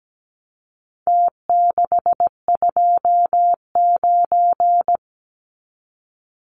Morse code, heard as a keyed tone, about 17 wpm, 700 Hz. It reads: T629
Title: Numbers - Contest Style - 4-Digit 17wpm